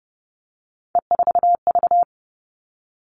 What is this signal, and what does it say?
Morse code, heard as a keyed tone, about 30 wpm, 700 Hz. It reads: E4V
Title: All Cut Numbers - 3-Digit 30wpm